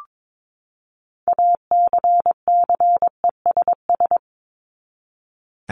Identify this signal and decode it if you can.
Morse code, heard as a keyed tone, about 22 wpm, 700 Hz. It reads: ACCESS
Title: Top 501-750 Words 22wpm